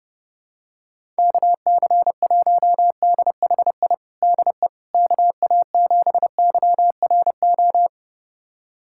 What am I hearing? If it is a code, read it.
Morse code, heard as a keyed tone, about 32 wpm, 700 Hz. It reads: KC1DHI DE KA7YRO
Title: USA Callsign DE Callsign - 6 Characters - 30wpm